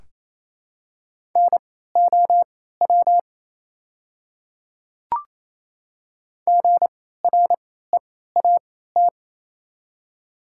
Morse code, heard as a keyed tone, about 28 wpm, spaced as for 16 wpm, 700 Hz. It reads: NOW GREAT
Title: 3x Character Spacing - Top 500 Words - 28wpm